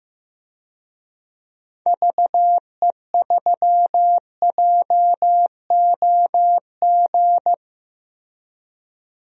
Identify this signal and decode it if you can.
Morse code, heard as a keyed tone, about 15 wpm, 700 Hz. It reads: VE3JOG
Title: DXCC-100 Call Signs - 6 Characters - Repeated 3x - 15wpm